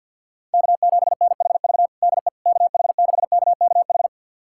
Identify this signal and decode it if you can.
Morse code, heard as a keyed tone, about 49 wpm, 700 Hz. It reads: K7NHV DE KH6KKH
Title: USA Callsign DE Callsign 50wpm